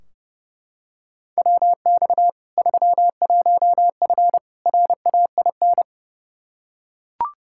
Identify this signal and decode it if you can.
Morse code, heard as a keyed tone, about 30 wpm, 700 Hz. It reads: WX 31F RAIN
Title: Practice QSO - Exchange 2 - Info - 30wpm